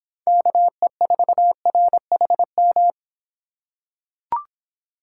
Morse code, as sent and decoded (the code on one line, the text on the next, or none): -.- . ....- .-. .... --
KE4RHM